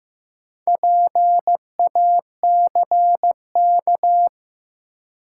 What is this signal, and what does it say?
Morse code, heard as a keyed tone, about 15 wpm, 700 Hz. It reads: PACK
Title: Sentences from Top 1300 Words - New Word Drill - 15wpm